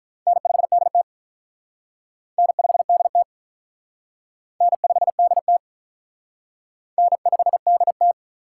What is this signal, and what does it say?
Morse code, heard as a keyed tone, about 44 wpm, 700 Hz. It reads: N5DT N5DT N5DT N5DT